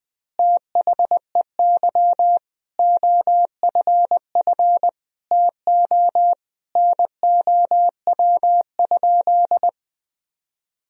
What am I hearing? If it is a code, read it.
Morse code, heard as a keyed tone, about 21 wpm, 700 Hz. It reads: THEY OFF TO NOW?
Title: Sentences from Top 200 Words and Abbreviations 20wpm